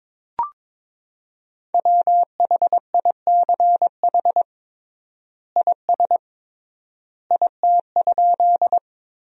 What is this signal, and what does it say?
Morse code, heard as a keyed tone, about 22 wpm, 700 Hz. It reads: WHICH IS IT?
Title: 3x Word-Spacing - Sentences from Top 100 Words - 22wpm